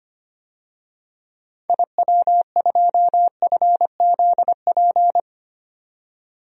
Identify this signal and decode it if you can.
Morse code, heard as a keyed tone, about 25 wpm, 700 Hz. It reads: IW2FZP
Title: DXCC-100 Call Signs - 6 Characters - 25wpm